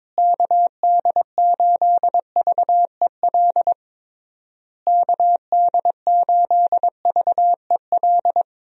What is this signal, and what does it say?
Morse code, heard as a keyed tone, about 22 wpm, 700 Hz. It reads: KD8VEL KD8VEL
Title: USA Call Signs - 6 Characters - Repeated 3x - 22wpm